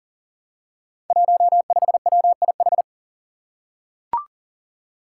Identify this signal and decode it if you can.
Morse code, heard as a keyed tone, about 40 wpm, 700 Hz. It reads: 15WIH
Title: Five Challenging Letter-Number Combinations 40wpm